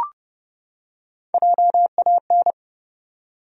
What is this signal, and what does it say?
Morse code, heard as a keyed tone, about 30 wpm, 700 Hz. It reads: JAN
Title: USA Common Names - Max 3 Letters - 30wpm